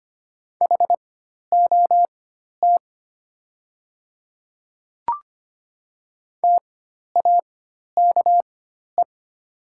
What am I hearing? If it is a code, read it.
Morse code, heard as a keyed tone, about 25 wpm, spaced as for 12 wpm, 700 Hz. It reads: HOT TAKE
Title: 4x Character Spacing - Top 500 Words - 25wpm